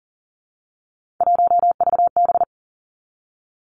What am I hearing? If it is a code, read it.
Morse code, heard as a keyed tone, about 39 wpm, 700 Hz. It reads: 1VB
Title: Three Challenging Letter-Number Combinations 40wpm